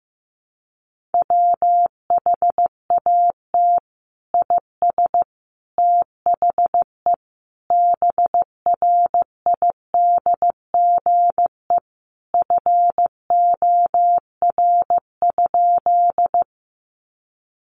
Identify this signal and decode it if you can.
Morse code, heard as a keyed tone, about 15 wpm, 700 Hz. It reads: WHAT IS THE BRIDGE FOR?
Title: Sentences from Top 1200 Words 15wpm